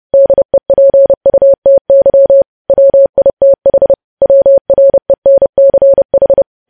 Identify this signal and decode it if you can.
Morse code, heard as a keyed tone, about 30 wpm, 550 Hz. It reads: DEPUTY WITH WRENCH
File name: podcast / News headlines-1x-30x30